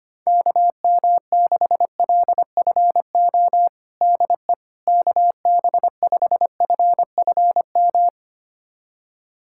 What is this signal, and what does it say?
Morse code, heard as a keyed tone, about 25 wpm, 700 Hz. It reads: KM6LFO DE KB5FFM
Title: USA Callsign DE Callsign - 6 Characters - 25wpm